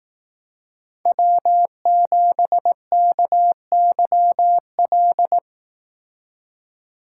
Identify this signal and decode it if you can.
Morse code, heard as a keyed tone, about 18 wpm, 700 Hz. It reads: W7KYL
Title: Speed Racing - USA 1x3 Call Signs 15wpm